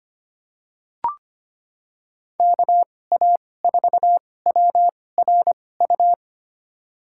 Morse code, heard as a keyed tone, about 25 wpm, spaced as for 18 wpm, 700 Hz. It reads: KA4WRU
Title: USA Call Signs - 6 Characters - 2x Character Spacing - 25wpm